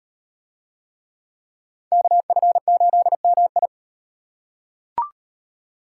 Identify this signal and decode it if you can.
Morse code, heard as a keyed tone, about 36 wpm, 700 Hz. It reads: KF8MI